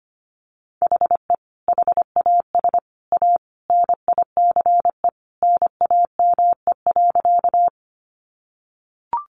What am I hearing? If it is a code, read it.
Morse code, heard as a keyed tone, about 25 wpm, 700 Hz. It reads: HE HAS A NICE NAME.
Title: Sentences from Top 1100 Words 25wpm